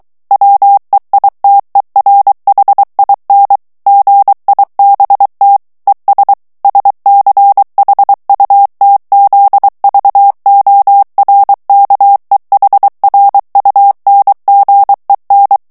Morse code, heard as a keyed tone, about 23 wpm, 800 Hz. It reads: WEITERHIN GIBT ES SCHUTZVORKEHRUNGEN
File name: morse mp3 / ALLEMAND texte clair 1 vitesse 22 mots minute